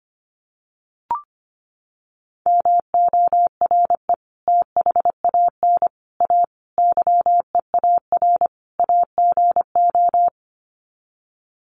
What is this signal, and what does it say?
Morse code, heard as a keyed tone, about 25 wpm, 700 Hz. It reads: MORE THAN A YEAR AGO
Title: Sets of 5 Words Encore2 25wpm